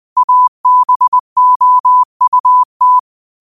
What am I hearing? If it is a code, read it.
Morse code, heard as a keyed tone, about 20 wpm, 1000 Hz. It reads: ABOUT
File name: qotdfiles / 20wpm